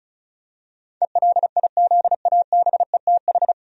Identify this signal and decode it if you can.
Morse code, heard as a keyed tone, about 36 wpm, 700 Hz. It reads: ELIZABETH
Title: USA Common Names - 35wpm